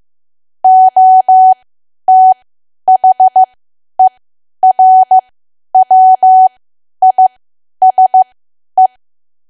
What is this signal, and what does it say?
Morse code, heard as a keyed tone, about 15 wpm, spaced as for 10 wpm, 750 Hz. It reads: OTHERWISE
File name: Archive / 10 WPM / 10WPM